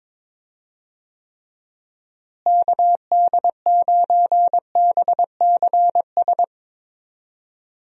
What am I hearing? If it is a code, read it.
Morse code, heard as a keyed tone, about 21 wpm, 700 Hz. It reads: KD9BCS
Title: USA Call Signs - 6 Characters - Repeated 3x - 22wpm